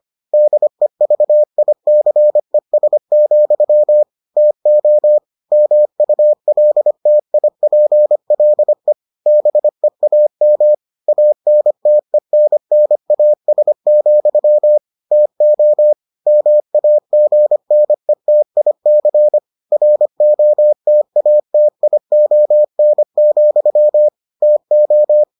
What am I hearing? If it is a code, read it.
Morse code, heard as a keyed tone, about 25 wpm, 600 Hz. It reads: DEVICES, TO MULTIPLE BEAM ANTENNAS, TO MAGNETIC ROTATION, TO